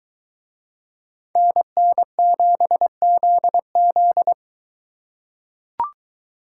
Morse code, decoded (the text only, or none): NN7ZZ